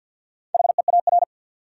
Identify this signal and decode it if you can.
Morse code, heard as a keyed tone, about 50 wpm, 700 Hz. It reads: HEAR
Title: Sets of 5 Words - New Word Drill - 50wpm